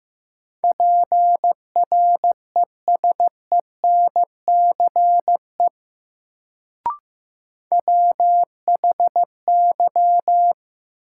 Morse code, heard as a keyed tone, about 15 wpm, 700 Hz. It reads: PRESENCE WHY
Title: Sets of 3 Words - New Word Drill - 15wpm